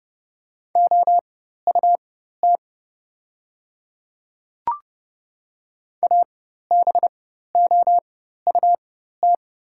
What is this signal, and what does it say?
Morse code, heard as a keyed tone, about 30 wpm, spaced as for 14 wpm, 700 Hz. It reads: OUT ABOUT
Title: 4x Character Spacing - Top 100 Words - 30wpm